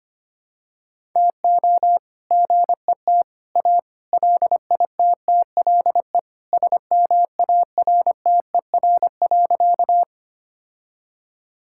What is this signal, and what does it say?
Morse code, heard as a keyed tone, about 25 wpm, 700 Hz. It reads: TO GET A LITTLE SMARTER.
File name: practice / 25wpm / All-Age Appropriate Jokes - 25wpm